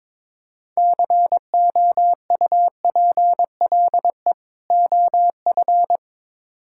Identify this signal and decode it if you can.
Morse code, heard as a keyed tone, about 22 wpm, 700 Hz. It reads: COUPLE OF
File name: Build A Phrase - 2-Word Phrases - 22wpm